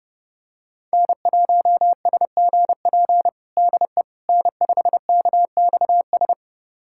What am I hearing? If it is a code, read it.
Morse code, heard as a keyed tone, about 30 wpm, 700 Hz. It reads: N1SGP DE N5KXS
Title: USA Callsign DE Callsign - 5 Characters - 30wpm